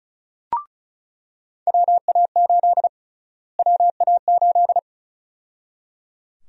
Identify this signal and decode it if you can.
Morse code, heard as a keyed tone, about 36 wpm, 700 Hz. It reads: WA8 WA8
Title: USA Call Sign Prefixes - Repeated 2x - 35wpm